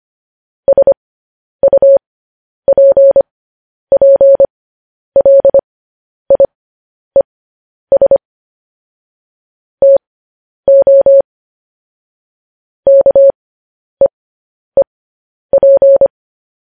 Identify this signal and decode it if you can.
Morse code, heard as a keyed tone, about 25 wpm, spaced as for 10 wpm, 550 Hz. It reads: SUPPLIES TO KEEP
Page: News Headlines in Morse Code at 10 WPM